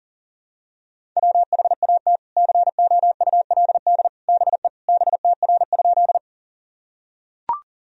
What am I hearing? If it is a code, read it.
Morse code, heard as a keyed tone, about 40 wpm, 700 Hz. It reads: WHAT COULD BE BTR?
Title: Sentences from Top 400 Words and Abbreviations 40wpm